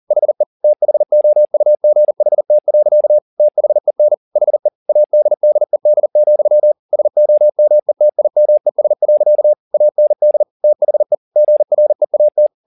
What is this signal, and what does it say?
Morse code, heard as a keyed tone, about 40 wpm, 600 Hz. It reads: HE THOUGHT. THEN HE ADDED, SOMETIMES. AND THE GREAT
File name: the old man and the sea 2 40wpm